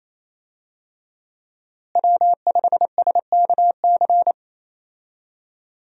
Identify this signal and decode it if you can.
Morse code, heard as a keyed tone, about 28 wpm, 700 Hz. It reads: W5SKC
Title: USA Call Signs - 5 Characters - Repeated 2x - 28wpm